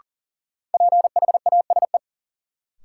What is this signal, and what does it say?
Morse code, heard as a keyed tone, about 39 wpm, 700 Hz. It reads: PHASE
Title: Sentences from Top 1200 Words - New Word Drill - 40wpm